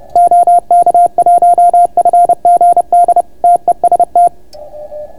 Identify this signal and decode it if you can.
Morse code, heard as a keyed tone, about 30 wpm, 690 Hz. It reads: OK1FGD TEST